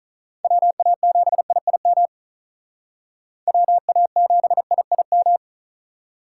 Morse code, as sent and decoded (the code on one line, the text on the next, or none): .-- .- --... .. .. -- / .-- .- --... .. .. --
WA7IIM WA7IIM